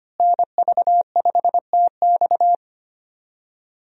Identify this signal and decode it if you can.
Morse code, heard as a keyed tone, about 25 wpm, 700 Hz. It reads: NV5TX
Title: USA Call Signs - 5 Characters - Repeated 2x - 25wpm